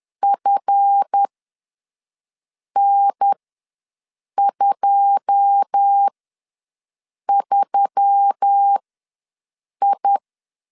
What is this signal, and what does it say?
Morse code, heard as a keyed tone, about 10 wpm, spaced as for 5 wpm, 800 Hz. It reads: FN23I
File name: LEZIONI / 9 2 ws 10wpm 4130